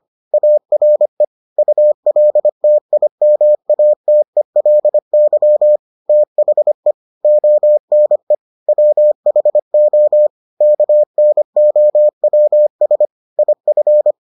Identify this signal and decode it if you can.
Morse code, heard as a keyed tone, about 25 wpm, 600 Hz. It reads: ARE ULTIMATELY THE ONE WHO KNOWS IF